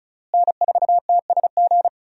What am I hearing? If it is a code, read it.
Morse code, heard as a keyed tone, about 36 wpm, 700 Hz. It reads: N4TSG
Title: USA Call Signs - 5 Characters - Repeated 3x - 35wpm